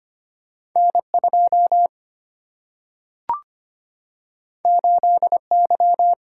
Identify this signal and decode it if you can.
Morse code, heard as a keyed tone, about 25 wpm, 700 Hz. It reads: N 2 8 Y
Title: Single-Letter Single-Number 25wpm